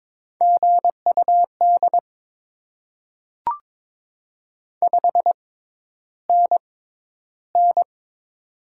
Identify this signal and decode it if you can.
Morse code, heard as a keyed tone, about 22 wpm, spaced as for 11 wpm, 690 Hz. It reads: GUD 5NN
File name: Top 20 QSO Elements - 6x Character Spacing - 22wpm